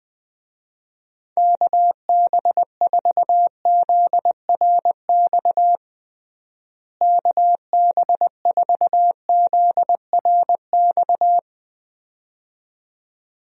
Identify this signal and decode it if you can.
Morse code, heard as a keyed tone, about 21 wpm, 700 Hz. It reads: KB4ZRX KB4ZRX
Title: USA Call Signs - 6 Characters - Repeated 2x - 20wpm